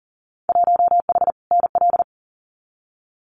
Morse code, heard as a keyed tone, about 39 wpm, 700 Hz. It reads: 1H NL
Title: Winter Field Day Exchanges 40wpm